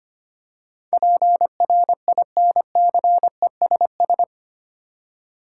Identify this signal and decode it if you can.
Morse code, heard as a keyed tone, about 25 wpm, 700 Hz. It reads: PRINCESS
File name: 8 Letter Words - First 3 Letters Given - 25wpm